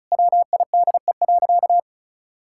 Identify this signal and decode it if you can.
Morse code, heard as a keyed tone, about 36 wpm, 700 Hz. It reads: WIDE.
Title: Sentences from Top 1000 Words - Review Part 6 - 35wpm